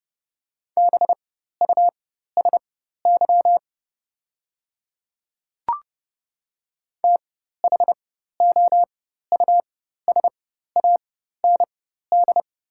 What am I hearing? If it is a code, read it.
Morse code, heard as a keyed tone, about 30 wpm, spaced as for 14 wpm, 700 Hz. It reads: BUSY THOUSAND
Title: 4x Character Spacing - Top 500 Words - 30wpm